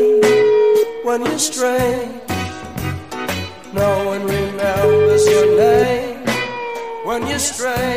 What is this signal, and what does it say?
Ringback tone (Europe, 425 Hz), with speech.